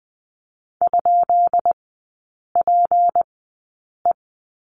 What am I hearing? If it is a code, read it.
Morse code, heard as a keyed tone, about 19 wpm, spaced as for 8 wpm, 690 Hz. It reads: ?PE
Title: 212 Lesson 34e - Number 8 - Three Characters 20wpm